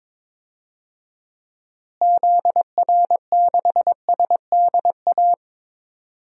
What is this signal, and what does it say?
Morse code, heard as a keyed tone, about 22 wpm, 700 Hz. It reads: ZR6SDA
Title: DXCC-100 Call Signs - 6 Characters - Repeated 3x - 22wpm